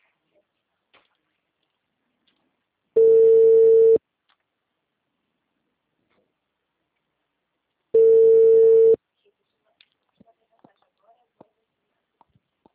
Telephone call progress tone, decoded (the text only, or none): ringback tone